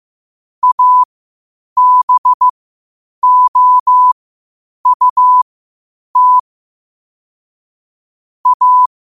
Morse code, heard as a keyed tone, about 15 wpm, spaced as for 8 wpm, 1000 Hz. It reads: ABOUT A
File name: qotdfiles / 8wpm